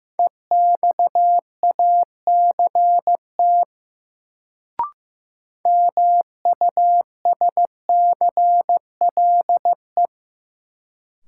Morse code, heard as a keyed tone, about 15 wpm, 700 Hz. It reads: EXACT MUSCLE
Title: Sentences from Top 1300 Words - New Word Drill Alt - 15wpm